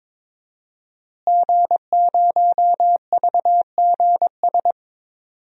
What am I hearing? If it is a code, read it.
Morse code, heard as a keyed tone, about 22 wpm, 700 Hz. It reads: G0VGS
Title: DXCC-100 Call Signs - 5 Characters - Repeated 3x - 22wpm